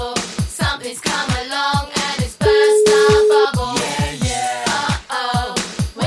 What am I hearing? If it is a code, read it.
Ringback tone (Europe, 425 Hz), with noise.